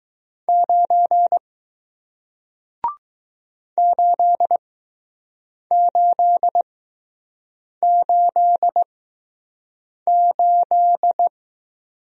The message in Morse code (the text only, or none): 9 8888